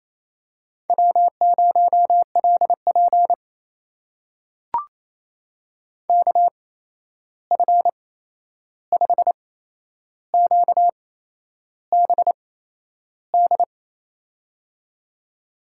Morse code, heard as a keyed tone, about 28 wpm, spaced as for 11 wpm, 700 Hz. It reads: W0LP KF5QBD
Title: USA Call Signs - 8x Character Spacing - 28wpm